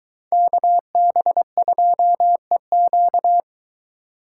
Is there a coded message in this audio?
Yes: Morse code from 0.3 to 3.4 s, heard as a keyed tone, about 22 wpm, 700 Hz. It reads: KB2EQ